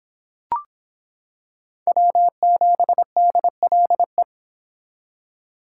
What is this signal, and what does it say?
Morse code, heard as a keyed tone, about 25 wpm, 700 Hz. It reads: W7DLE